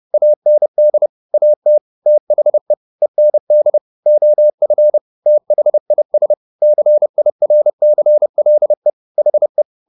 Morse code, heard as a keyed tone, about 30 wpm, 600 Hz. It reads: AND AT THE END OF THIS CIRCLE HE